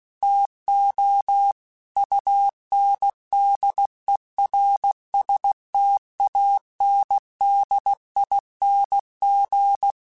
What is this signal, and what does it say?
Morse code, heard as a keyed tone, about 16 wpm, 800 Hz. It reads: TO UNDERSTANDING
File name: Technical Subjects / Technical Subjects-mp3s / Sw Pwr Sup @15